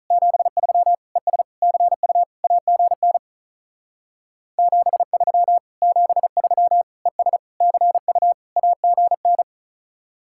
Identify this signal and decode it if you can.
Morse code, heard as a keyed tone, about 42 wpm, 700 Hz. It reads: 73 ES CU AGN 73 73 ES CU AGN
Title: Speed-Racing Common QSO Phrases - 35wpm